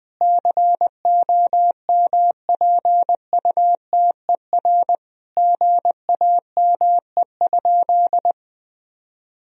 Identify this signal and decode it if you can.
Morse code, heard as a keyed tone, about 20 wpm, 700 Hz. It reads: COMPUTER GAME?
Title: Sentences from Top 1000 Words - Review Part 3 - 20wpm